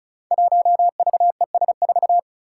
Morse code, heard as a keyed tone, about 36 wpm, 700 Hz. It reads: 1VES4
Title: Five Challenging Letter-Number Combinations 35wpm